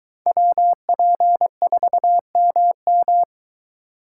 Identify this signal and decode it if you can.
Morse code, heard as a keyed tone, about 22 wpm, 700 Hz. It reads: WP4MM